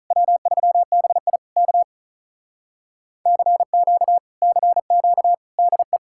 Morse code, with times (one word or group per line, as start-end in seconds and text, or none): 0.1-1.4 W3BI
1.6-1.8 K
3.3-4.2 CQ
4.4-5.3 CQ
5.6-6.0 DE